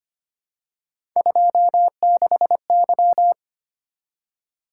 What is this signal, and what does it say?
Morse code, heard as a keyed tone, about 25 wpm, 700 Hz. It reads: 26Y
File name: Three Letter-Number Combinations - 25wpm